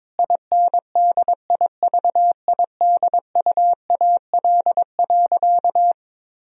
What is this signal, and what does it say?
Morse code, heard as a keyed tone, about 22 wpm, 700 Hz. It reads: INDIVIDUAL.
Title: Sentences from Top 1000 Words - Review Part 4 - 22wpm